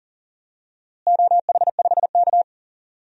Morse code, heard as a keyed tone, about 37 wpm, 700 Hz. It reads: OH5K
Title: DXCC-100 Call Signs - 4 Characters - Repeated 2x - 40wpm